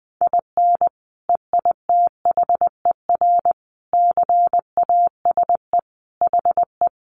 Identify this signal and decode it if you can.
Morse code, heard as a keyed tone, about 20 wpm, 700 Hz. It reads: IN EITHER CASE HE
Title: Sentences from Top 1300 Words 20wpm